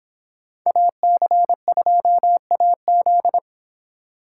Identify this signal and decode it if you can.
Morse code, heard as a keyed tone, about 29 wpm, 700 Hz. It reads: AC2AZ